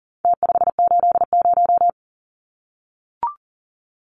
Morse code, heard as a keyed tone, about 40 wpm, 700 Hz. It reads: T580
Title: Numbers - Contest Style - 4-Digit 40wpm